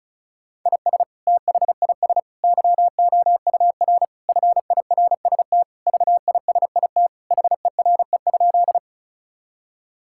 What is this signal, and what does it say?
Morse code, heard as a keyed tone, about 36 wpm, 700 Hz. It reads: IS THIS YOUR FIRST VISIT HERE?
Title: Sentences from Top 1000 Words - Review Part 6 - 35wpm